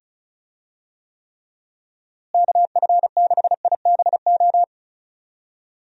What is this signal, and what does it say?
Morse code, heard as a keyed tone, about 34 wpm, 700 Hz. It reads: KF6IBO
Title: USA Call Signs - 6 Characters - Repeated 2x - 35wpm